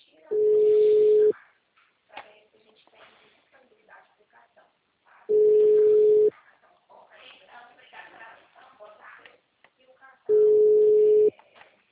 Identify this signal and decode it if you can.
Ringback tone (Europe, 425 Hz), with speech.
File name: out- -2020- - - .84020